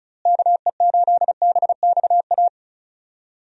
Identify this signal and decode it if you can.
Morse code, heard as a keyed tone, about 34 wpm, 700 Hz. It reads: KE8BXA
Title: USA Call Signs - 6 Characters - Repeated 3x - 35wpm